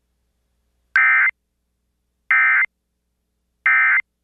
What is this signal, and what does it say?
Emergency Alert System (SAME) tones, heard as harsh buzzing bursts. It is heard from 0.9 to 4.0 s.